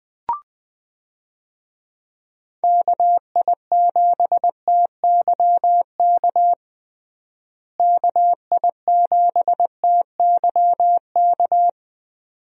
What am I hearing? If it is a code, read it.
Morse code, heard as a keyed tone, about 20 wpm, 700 Hz. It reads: KI7TYK KI7TYK
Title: USA Call Signs - 6 Characters - Repeated 2x - 20wpm